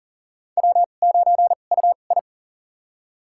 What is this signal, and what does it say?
Morse code, heard as a keyed tone, about 40 wpm, spaced as for 29 wpm, 700 Hz. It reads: W9UI
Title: USA Call Signs - 4 Characters - 2x Character Spacing - 40wpm